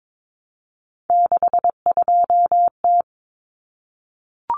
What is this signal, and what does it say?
Morse code, heard as a keyed tone, about 22 wpm, 700 Hz. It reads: 62T
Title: Common Cut Numbers - 3-Digit 22wpm